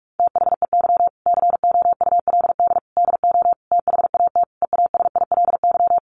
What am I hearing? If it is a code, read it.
Morse code, heard as a keyed tone, about 44 wpm, 700 Hz. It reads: THEY COULD DO THAT EASILY
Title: Sentences from Top 900 Words 45wpm